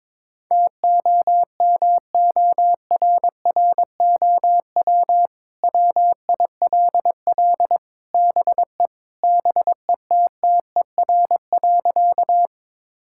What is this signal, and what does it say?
Morse code, heard as a keyed tone, about 22 wpm, 700 Hz. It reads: TOMORROW WILL BE BETTER.
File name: practice / 22wpm / Sentences with 2-Word Phrases - 22wpm